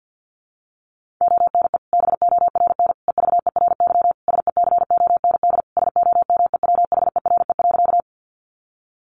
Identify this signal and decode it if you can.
Morse code, heard as a keyed tone, about 50 wpm, 700 Hz. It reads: ONE BORN EVERY SECOND SOMEWHERE.
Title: Sentences from Top 1400 Words 50wpm